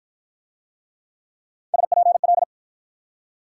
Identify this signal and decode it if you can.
Morse code, heard as a keyed tone, about 48 wpm, 690 Hz. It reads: SPL